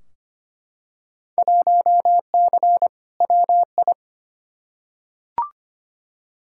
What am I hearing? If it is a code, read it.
Morse code, heard as a keyed tone, about 25 wpm, 700 Hz. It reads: 1C WI
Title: Field Day Exchanges - 25wpm